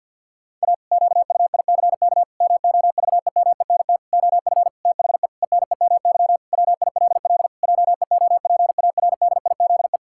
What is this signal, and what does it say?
Morse code, heard as a keyed tone, about 50 wpm, 700 Hz. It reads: A QUICK MOVEMENT OF THE ENEMY WILL JEOPARDIZE
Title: Pangram Challenge v2 - 50wpm